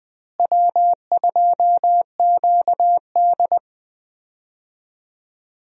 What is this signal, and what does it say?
Morse code, heard as a keyed tone, about 20 wpm, 700 Hz. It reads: W2QD
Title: USA Call Signs - 4 Characters - Repeated 3x - 20wpm